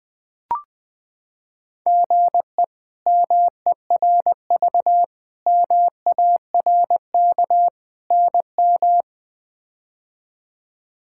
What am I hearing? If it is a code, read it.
Morse code, heard as a keyed tone, about 20 wpm, 700 Hz. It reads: GE MERV MARK NM
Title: K1USN SST QSO Practice - 20wpm